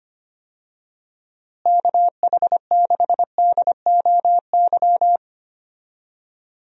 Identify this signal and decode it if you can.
Morse code, heard as a keyed tone, about 25 wpm, 700 Hz. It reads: KH6DOY
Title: DXCC-100 Call Signs - 6 Characters - Repeated 2x - 25wpm